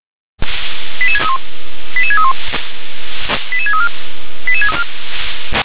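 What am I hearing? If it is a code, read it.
Five-tone selective call (ZVEI), heard as quick runs of five tones, tones of about 70 ms. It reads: 88631 88631 88634
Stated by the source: Five-tone ZVEI selective call received by SDR.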